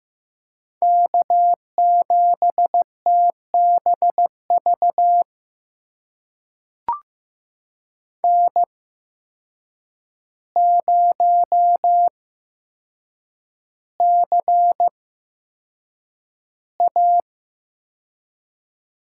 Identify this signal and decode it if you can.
Morse code, heard as a keyed tone, about 15 wpm, spaced as for 7 wpm, 700 Hz. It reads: K7TBV N0CA
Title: USA Call Signs - 5 Characters - 8x Character Spacing - 15wpm